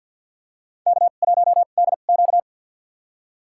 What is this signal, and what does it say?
Morse code, heard as a keyed tone, about 51 wpm, spaced as for 36 wpm, 700 Hz. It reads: K1DQ
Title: USA Call Signs - 4 Characters - 2x Character Spacing - 50wpm